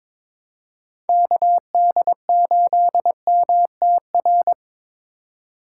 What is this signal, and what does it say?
Morse code, heard as a keyed tone, about 22 wpm, 700 Hz. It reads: KD8MTR
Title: USA Call Signs - 6 Characters - 2x Character Spacing - 22wpm